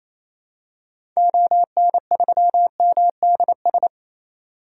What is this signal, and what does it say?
Morse code, heard as a keyed tone, about 28 wpm, 700 Hz. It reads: ON3MDS
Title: DXCC-100 Call Signs - 6 Characters - Repeated 2x - 28wpm